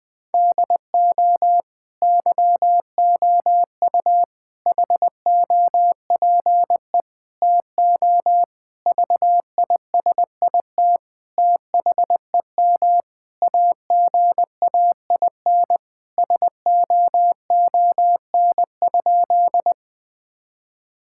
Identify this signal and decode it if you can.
Morse code, heard as a keyed tone, about 20 wpm, 700 Hz. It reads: DO YOU HOPE TO VISIT THEM AGAIN SOON?
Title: Sentences from Top 1000 Words - Review Part 6 - 20wpm